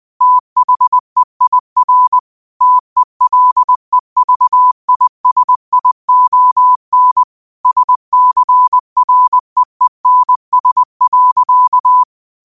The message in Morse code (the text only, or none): THEIR TELEVISION SCREENS.